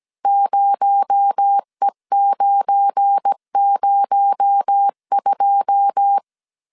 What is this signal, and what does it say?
Morse code, heard as a keyed tone, about 17 wpm, 800 Hz. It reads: 0E902
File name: CW / DET / 1 9 1500C S A 16wpm 2828